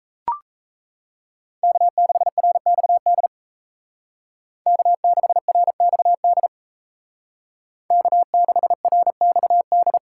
Morse code, heard as a keyed tone, about 33 wpm, 700 Hz. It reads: K6RXD K6RXD K6RXD
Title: Speed Racing - USA 1x3 Call Signs 28wpm